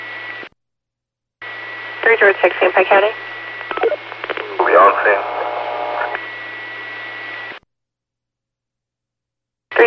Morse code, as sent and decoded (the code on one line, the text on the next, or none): none